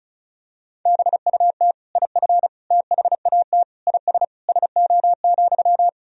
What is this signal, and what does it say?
Morse code, heard as a keyed tone, about 34 wpm, 690 Hz. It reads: BUT IF THAT IS SO,
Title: Sentences from Top 200 Words and Abbreviations 35wpm